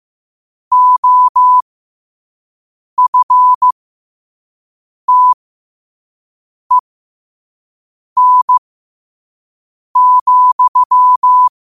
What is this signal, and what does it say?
Morse code, heard as a keyed tone, about 15 wpm, spaced as for 5 wpm, 1000 Hz. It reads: OFTEN,
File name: qotdfiles / 5wpm